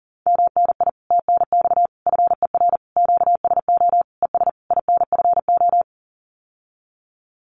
Morse code, heard as a keyed tone, about 40 wpm, 700 Hz. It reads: MNI TNX FER QSO ES INFO
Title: Common QSO Phrases Encore 40wpm